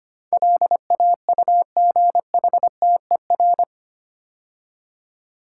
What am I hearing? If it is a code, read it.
Morse code, heard as a keyed tone, about 25 wpm, 700 Hz. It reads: LAUGHTER
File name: 8 Letter Words - First 3 Letters Given - 25wpm